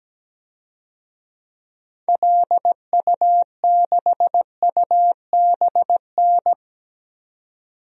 Morse code, heard as a keyed tone, about 17 wpm, 700 Hz. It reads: LU6UBN
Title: DXCC-100 Call Signs - 6 Characters - 17wpm